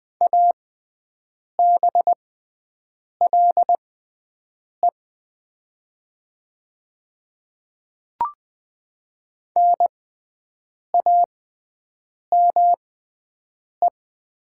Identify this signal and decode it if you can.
Morse code, heard as a keyed tone, about 20 wpm, spaced as for 7 wpm, 700 Hz. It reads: ABLE NAME
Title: 6x Character Spacing - Top 500 Words - 20wpm